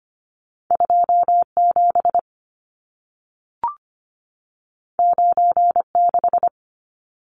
Morse code, heard as a keyed tone, about 25 wpm, 700 Hz. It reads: 27 96
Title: Numbers - 2-Digit 25wpm